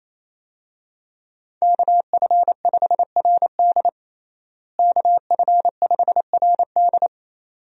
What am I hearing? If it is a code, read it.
Morse code, heard as a keyed tone, about 28 wpm, 700 Hz. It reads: KF5RD KF5RD
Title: USA Call Signs - 5 Characters - Repeated 2x - 28wpm